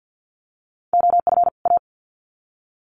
Morse code, heard as a keyed tone, about 50 wpm, 700 Hz. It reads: OF A
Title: Sets of 2 Words 50wpm